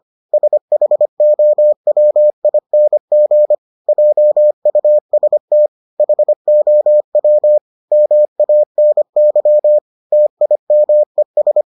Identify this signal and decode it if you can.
Morse code, heard as a keyed tone, about 25 wpm, 600 Hz. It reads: SHOWING JUST HOW MANY TIMES